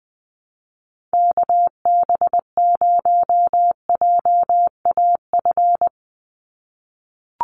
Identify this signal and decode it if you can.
Morse code, heard as a keyed tone, about 20 wpm, 700 Hz. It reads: KB0JAF
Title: USA Call Signs 20wpm